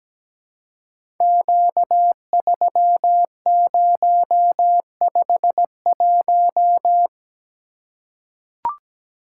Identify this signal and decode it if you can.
Morse code, heard as a keyed tone, about 17 wpm, 700 Hz. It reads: Q3051
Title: Code Groups - 5 Letter:Numbers - 17wpm